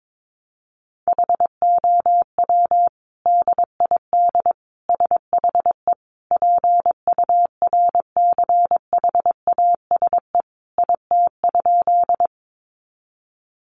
Morse code, heard as a keyed tone, about 22 wpm, 700 Hz. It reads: HOW DID SHE PURCHASE IT?
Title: Sentences from Top 800 Words 22wpm